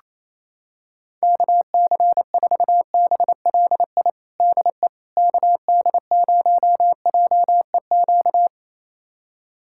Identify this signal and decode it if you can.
Morse code, heard as a keyed tone, about 28 wpm, 700 Hz. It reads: KC4BLI DE KD0JEQ
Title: USA Callsign DE Callsign 28wpm